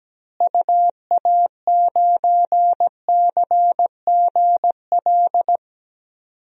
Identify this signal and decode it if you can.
Morse code, heard as a keyed tone, about 17 wpm, 700 Hz. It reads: UA9CGL
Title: DXCC-100 Call Signs - 6 Characters - Repeated 3x - 17wpm